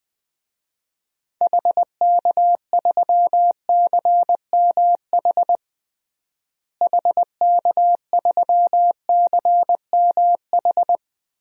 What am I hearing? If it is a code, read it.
Morse code, heard as a keyed tone, about 20 wpm, 700 Hz. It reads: HK3CMH HK3CMH
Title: DXCC-100 Call Signs - 6 Characters - Repeated 3x - 20wpm